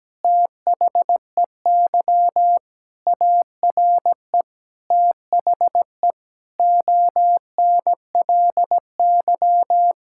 Morse code, heard as a keyed tone, about 17 wpm, 700 Hz. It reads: THEY ARE THE ONLY
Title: Sentences from Top 1000 Words - Review Part 6 - 17wpm